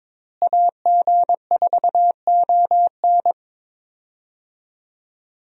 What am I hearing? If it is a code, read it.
Morse code, heard as a keyed tone, about 22 wpm, 700 Hz. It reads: AG4ON
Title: USA Call Signs - 5 Characters - Repeated 2x - 22wpm